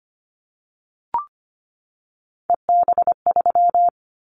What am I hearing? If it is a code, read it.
Morse code, heard as a keyed tone, about 25 wpm, 700 Hz. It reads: EB3
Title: Callsign Prefixes for Top 25 DXCC - 25wpm